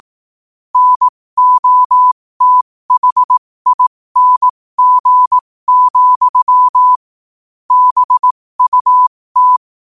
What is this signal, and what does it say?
Morse code, heard as a keyed tone, about 18 wpm, spaced as for 15 wpm, 1000 Hz. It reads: NOTHING, BUT
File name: qotdfiles / 15wpm